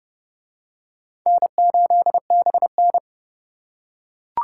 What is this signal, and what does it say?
Morse code, heard as a keyed tone, about 29 wpm, 700 Hz. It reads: N8BN